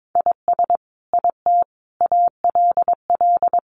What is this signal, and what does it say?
Morse code, heard as a keyed tone, about 22 wpm, 700 Hz. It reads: IS IT ALL
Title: Sentences from Top 200 Words 22wpm